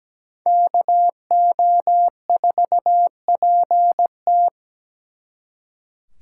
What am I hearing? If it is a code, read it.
Morse code, heard as a keyed tone, about 20 wpm, 700 Hz. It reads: KO4PT
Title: Speed Racing - USA Call Signs 17wpm